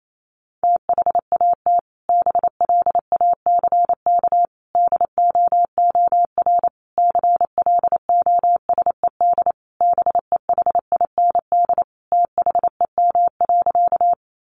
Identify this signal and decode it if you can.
Morse code, heard as a keyed tone, about 28 wpm, 700 Hz. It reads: THAT BLACK DOOR CLOSED BEHIND THEM.
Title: Sentences from Top 1400 Words 28wpm